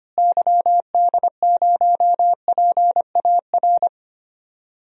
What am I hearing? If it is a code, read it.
Morse code, heard as a keyed tone, about 25 wpm, 700 Hz. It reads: YD0PAR
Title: DXCC-100 Call Signs - 6 Characters - Repeated 2x - 25wpm